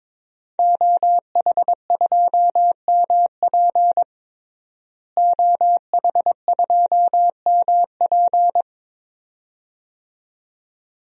Morse code, heard as a keyed tone, about 22 wpm, 700 Hz. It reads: OH2MP OH2MP
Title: DXCC-100 Call Signs - 5 Characters - Repeated 2x - 22wpm